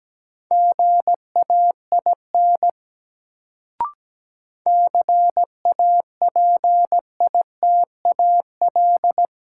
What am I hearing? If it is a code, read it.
Morse code, heard as a keyed tone, about 17 wpm, 700 Hz. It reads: GAIN CAPITAL
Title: Sentences from Top 1000 Words - New Word Drill - 17wpm